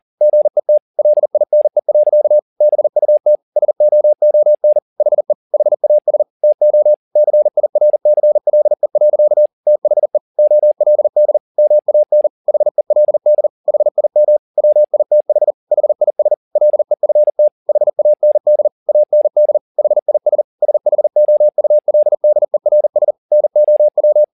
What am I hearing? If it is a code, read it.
Morse code, heard as a keyed tone, about 40 wpm, 600 Hz. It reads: GET LINE. BUT SOON HE HAS TO CIRCLE. THE OLD MAN HELD HIM WITH HIS LEFT HAND AND HIS SHOULDERS NOW
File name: the old man and the sea 2 40wpm